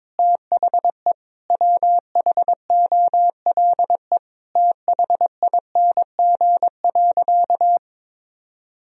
Morse code, heard as a keyed tone, about 22 wpm, 700 Hz. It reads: THE WHOLE THING.
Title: Sentences from Top 1000 Words - Review Part 1 - 22wpm